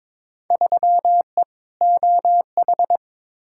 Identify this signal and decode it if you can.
Morse code, heard as a keyed tone, about 22 wpm, 700 Hz. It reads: 3E OH
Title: Field Day Exchanges - 22wpm